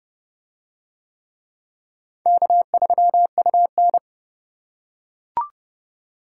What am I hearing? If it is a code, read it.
Morse code, heard as a keyed tone, about 29 wpm, 700 Hz. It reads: K3UN